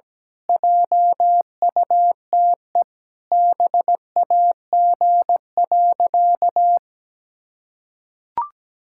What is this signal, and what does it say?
Morse code, heard as a keyed tone, about 17 wpm, 700 Hz. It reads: JUTE BAG.
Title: Pangram Challenge v2 - 17wpm